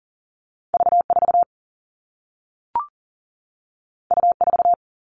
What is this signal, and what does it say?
Morse code, heard as a keyed tone, about 42 wpm, 700 Hz. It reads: V4 U4
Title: All Cut Numbers - 2-Digit 40wpm